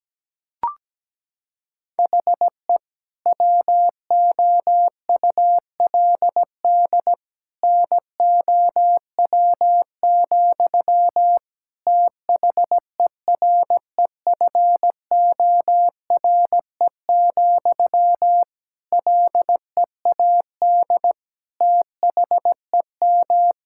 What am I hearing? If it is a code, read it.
Morse code, heard as a keyed tone, about 17 wpm, 700 Hz. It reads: HE WOULD NOW, THEREFORE, LEAD THEM
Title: Sentences from Top 1000 Words - Review Part 5 - 17wpm